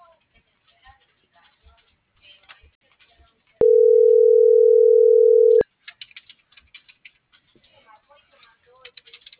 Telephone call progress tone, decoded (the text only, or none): ringback tone